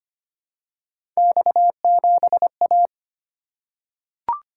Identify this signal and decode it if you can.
Morse code, heard as a keyed tone, about 25 wpm, 700 Hz. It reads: X7A
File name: Three Letter-Number Combinations - 25wpm